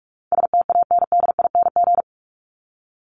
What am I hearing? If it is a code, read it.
Morse code, heard as a keyed tone, about 44 wpm, 700 Hz. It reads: STANDING
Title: ING Suffix - Top 100 Words with Base Words 45wpm